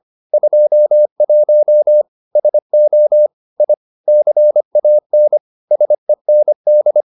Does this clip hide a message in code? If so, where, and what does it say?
Yes: Morse code from 0.3 to 7.0 s, heard as a keyed tone, about 25 wpm, 600 Hz. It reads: 21 SO I CAN SEND